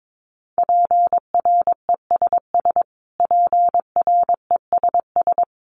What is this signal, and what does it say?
Morse code, heard as a keyed tone, about 22 wpm, 700 Hz. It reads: PRESS PRESS
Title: Top 500 Words Repeated 22wpm